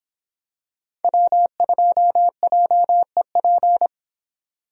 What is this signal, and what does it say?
Morse code, heard as a keyed tone, about 26 wpm, 700 Hz. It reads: W2JEP